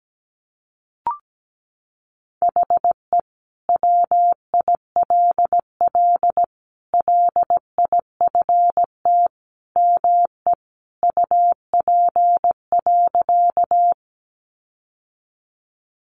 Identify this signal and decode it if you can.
Morse code, heard as a keyed tone, about 17 wpm, 700 Hz. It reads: HE WILL LIFT ME UP.
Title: Sentences from Top 1000 Words 17wpm